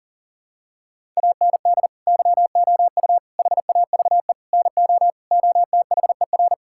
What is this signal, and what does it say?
Morse code, heard as a keyed tone, about 40 wpm, 700 Hz. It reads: AND YOU HAVE NO OTHER
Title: Sentences from Top 1000 Words - Review Part 6 - 40wpm